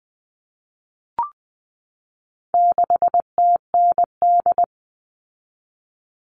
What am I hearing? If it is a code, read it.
Morse code, heard as a keyed tone, about 20 wpm, 700 Hz. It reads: 6TND
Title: All Cut Numbers - 4-Digit 20wpm